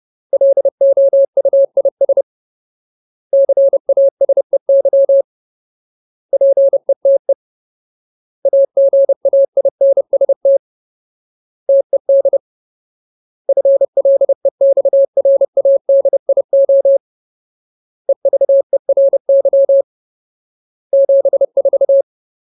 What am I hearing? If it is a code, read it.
Morse code, heard as a keyed tone, about 30 wpm, 550 Hz. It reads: LOUIS CASEY PETE AGAINST TED FLEXRADIO EVERY 74
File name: Ultimate Word List / 30WPM Ultimate